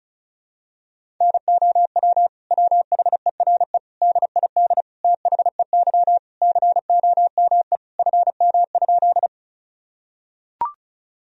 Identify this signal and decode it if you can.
Morse code, heard as a keyed tone, about 34 wpm, 700 Hz. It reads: NOW WHERE DID THEY COME FM?
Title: Sentences from Top 200 Words and Abbreviations 35wpm